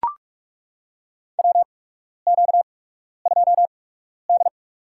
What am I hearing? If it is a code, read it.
Morse code, heard as a keyed tone, about 42 wpm, spaced as for 12 wpm, 690 Hz. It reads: WQ2D
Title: USA Call Signs - 4 Characters - 8x Character Spacing - 45wpm